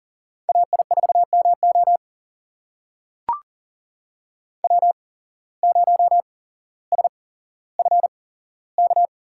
Morse code, heard as a keyed tone, about 38 wpm, spaced as for 17 wpm, 700 Hz. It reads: AI4MO W0SFK
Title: USA Call Signs - 5 Characters - 8x Character Spacing - 40wpm